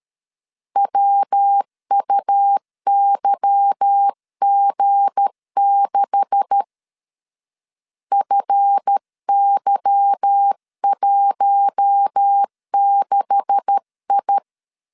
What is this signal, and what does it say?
Morse code, heard as a keyed tone, about 13 wpm, 800 Hz. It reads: WUYG6 FY16I
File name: LEZIONI / 9 286 12wpm 2305